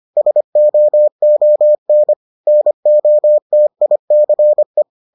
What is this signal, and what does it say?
Morse code, heard as a keyed tone, about 25 wpm, 600 Hz. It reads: SOON NOTICE